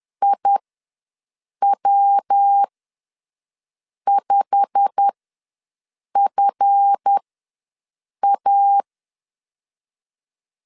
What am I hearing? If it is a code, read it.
Morse code, heard as a keyed tone, about 10 wpm, spaced as for 6 wpm, 800 Hz. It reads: IW5FA
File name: LEZIONI / 8 YJ45 / 8 YJ45 ws 10wpm 3843